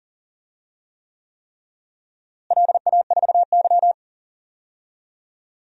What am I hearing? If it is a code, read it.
Morse code, heard as a keyed tone, about 40 wpm, 700 Hz. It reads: LA4Y
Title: DXCC-100 Call Signs - 4 Characters - Repeated 3x - 40wpm